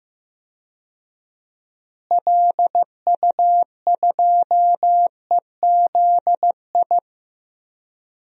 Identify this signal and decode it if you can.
Morse code, heard as a keyed tone, about 15 wpm, 700 Hz. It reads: LU2EZI
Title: DXCC-100 Call Signs - 6 Characters - Repeated 2x - 15wpm